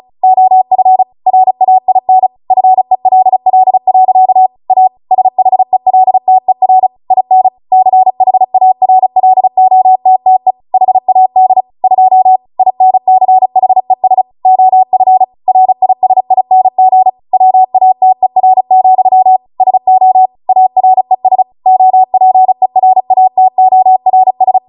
Morse code, heard as a keyed tone, about 36 wpm, 750 Hz. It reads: OF RAIN FELL. A SHELTER IN CHARLOTTE HAD 2 INCHES OF RISING WATER, SO ARES OPERATORS
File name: Morse / 35WPM